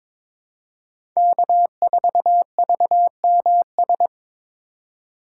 Morse code, heard as a keyed tone, about 22 wpm, 700 Hz. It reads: K4VMS
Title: USA Call Signs - 5 Characters - 2x Character Spacing - 22wpm